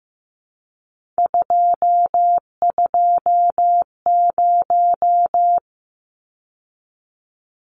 Morse code, heard as a keyed tone, about 15 wpm, 700 Hz. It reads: 220
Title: Numbers - 3-Digit 15wpm